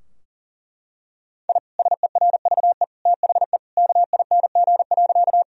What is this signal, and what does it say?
Morse code, heard as a keyed tone, about 40 wpm, 700 Hz. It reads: I SERVE THE KING.
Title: Sentences from Top 1000 Words - Review Part 2 - 40wpm